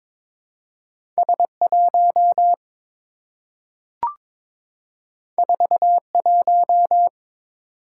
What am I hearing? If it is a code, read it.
Morse code, heard as a keyed tone, about 23 wpm, 700 Hz. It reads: S1 41
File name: practice / 22wpm / Two Letter-Number Combinations - 22wpm